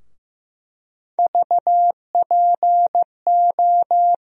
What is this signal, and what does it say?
Morse code, heard as a keyed tone, about 15 wpm, 700 Hz. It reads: VPO
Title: USA Call Sign Postfixes - 3 Letters 15wpm